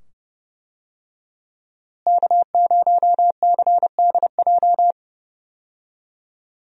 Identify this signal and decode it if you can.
Morse code, heard as a keyed tone, about 29 wpm, 700 Hz. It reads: K0CDJ
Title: USA Call Signs - 5 Characters - Repeated 2x - 30wpm